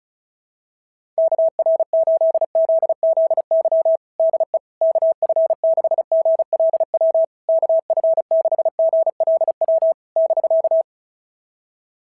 Morse code, heard as a keyed tone, about 34 wpm, 650 Hz. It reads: KR8ZZY DE KF6GLW KF6GLW <BK>
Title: SOTA QSO Practice 35wpm